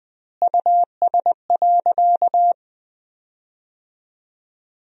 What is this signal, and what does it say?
Morse code, heard as a keyed tone, about 20 wpm, 700 Hz. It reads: US.
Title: 2x Word-Spacing - Sentences from Top 500 Words - 20wpm